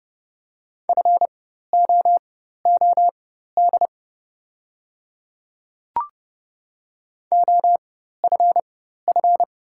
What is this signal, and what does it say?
Morse code, heard as a keyed tone, about 30 wpm, spaced as for 14 wpm, 700 Hz. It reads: FOOD OFF
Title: 4x Character Spacing - Top 500 Words - 30wpm